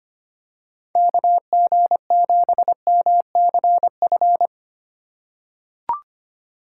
Morse code, heard as a keyed tone, about 25 wpm, 700 Hz. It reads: KG7MCF
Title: USA Call Signs - 6 Characters - 4x Character Spacing - 25wpm